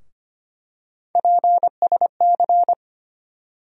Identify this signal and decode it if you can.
Morse code, heard as a keyed tone, about 25 wpm, 700 Hz. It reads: PSC
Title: USA Call Sign Postfixes - 3 Letters 25wpm